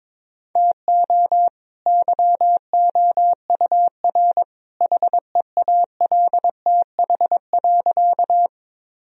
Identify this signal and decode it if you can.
Morse code, heard as a keyed tone, about 22 wpm, 700 Hz. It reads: TO YOUR HEALTH.
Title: Sentences from Top 1000 Words - Review Part 3 - 22wpm